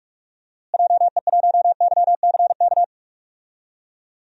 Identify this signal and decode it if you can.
Morse code, heard as a keyed tone, about 45 wpm, 700 Hz. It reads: JE1YCK
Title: DXCC-100 Call Signs - 6 Characters - Repeated 3x - 45wpm